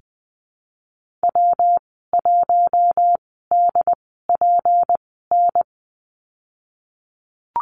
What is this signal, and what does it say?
Morse code, heard as a keyed tone, about 20 wpm, spaced as for 15 wpm, 700 Hz. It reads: W1DPN
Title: 186 Lesson 30g - Letter b - Call Signs 20wpm